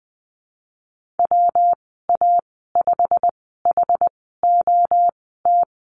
Morse code, heard as a keyed tone, about 19 wpm, spaced as for 14 wpm, 700 Hz. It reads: WA5HOT
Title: 186 Lesson 30g - Letter b - Call Signs 20wpm